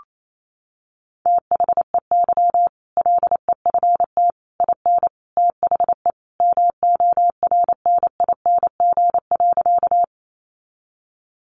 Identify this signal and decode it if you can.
Morse code, heard as a keyed tone, about 28 wpm, 700 Hz. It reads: THEY LEFT IN THE MORNING.
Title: Sentences from Top 400 Words 28wpm